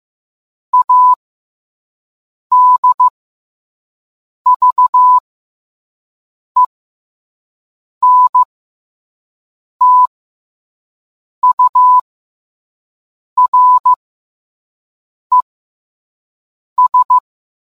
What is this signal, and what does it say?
Morse code, heard as a keyed tone, about 15 wpm, spaced as for 5 wpm, 1000 Hz. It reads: ADVENTURES